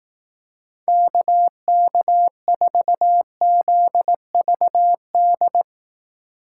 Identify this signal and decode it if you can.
Morse code, heard as a keyed tone, about 18 wpm, 700 Hz. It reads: KK4ZVD
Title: Speed Racing - USA 2x3 Call Signs 15wpm